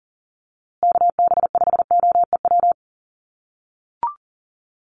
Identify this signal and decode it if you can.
Morse code, heard as a keyed tone, about 40 wpm, 700 Hz. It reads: KB5OEW
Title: USA Call Signs Repeated 3x 40wpm